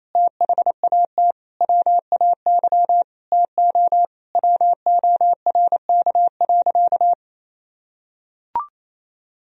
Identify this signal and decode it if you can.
Morse code, heard as a keyed tone, about 28 wpm, 700 Hz. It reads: THAT WAY TO WORK.
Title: Sentences from Top 1000 Words - Review Part 3 - 28wpm